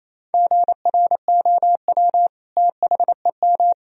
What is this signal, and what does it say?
Morse code, heard as a keyed tone, about 28 wpm, 700 Hz. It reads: GROW THEM
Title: Sentences from Top 1000 Words - Review Part 1 - 28wpm